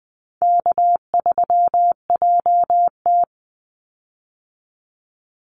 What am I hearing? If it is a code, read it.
Morse code, heard as a keyed tone, about 20 wpm, 700 Hz. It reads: K3JT
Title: Easy Call Signs - 20wpm